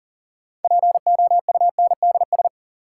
Morse code, heard as a keyed tone, about 39 wpm, 700 Hz. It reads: POUNDS
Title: Sentences from Top 1000 Words - Review Part 2 - New Word Drill - 40wpm